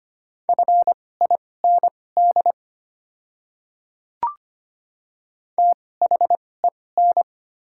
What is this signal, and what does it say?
Morse code, heard as a keyed tone, about 25 wpm, spaced as for 18 wpm, 690 Hz. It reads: FIND THEN
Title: 2x Character Spacing - Top 100 Words - 25wpm